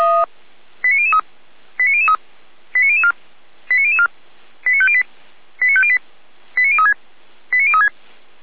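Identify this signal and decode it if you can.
Five-tone selective call (ZVEI), heard as quick runs of five tones, tones of about 70 ms. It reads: 89003 89003 89004 89004 89508 89508 89036 89036
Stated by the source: ZVEI five-tone selective call, code 89010, received via SDR by fire brigade Feuerwehr Orlamünde.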